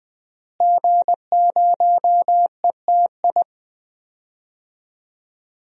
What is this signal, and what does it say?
Morse code, heard as a keyed tone, about 20 wpm, 700 Hz. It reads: G0ETI
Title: DXCC-100 Call Signs - 5 Characters - Repeated 3x - 20wpm